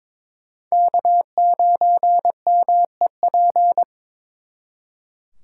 Morse code, heard as a keyed tone, about 26 wpm, 700 Hz. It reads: K9MEP
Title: Speed Racing - USA 1x3 Call Signs 22wpm